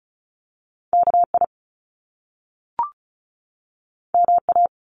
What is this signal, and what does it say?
Morse code, heard as a keyed tone, about 34 wpm, 700 Hz. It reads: K I M A
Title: Top Two Letter Combinations in Words 35wpm